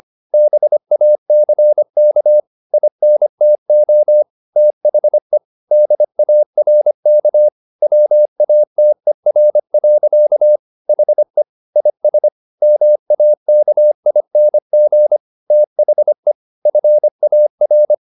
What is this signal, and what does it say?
Morse code, heard as a keyed tone, about 25 wpm, 600 Hz. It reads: BACK INTO THE DARK WATER. HE IS MAKING THE FAR